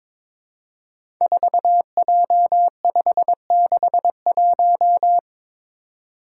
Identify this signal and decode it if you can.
Morse code, heard as a keyed tone, about 22 wpm, 700 Hz. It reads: 4J561
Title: Five Challenging Letter-Number Combinations 22wpm